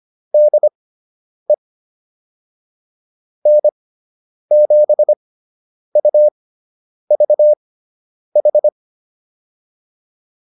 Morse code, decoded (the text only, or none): DE N7UVH